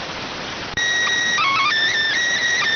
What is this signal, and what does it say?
SSTV (slow-scan television), PD 290.